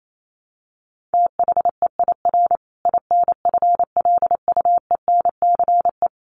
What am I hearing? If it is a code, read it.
Morse code, heard as a keyed tone, about 28 wpm, 700 Hz. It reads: THEIR INFLUENCE
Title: Sentences from Top 1400 Words 28wpm